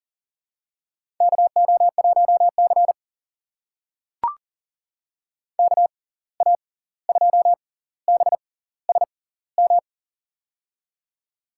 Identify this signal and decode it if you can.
Morse code, heard as a keyed tone, about 40 wpm, spaced as for 18 wpm, 700 Hz. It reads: KO1C KA2BSM
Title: USA Call Signs - 6x Character Spacing - 40wpm